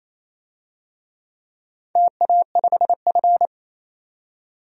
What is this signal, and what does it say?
Morse code, heard as a keyed tone, about 27 wpm, 700 Hz. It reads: TA5F
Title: DXCC-100 Call Signs - 4 Characters - 28wpm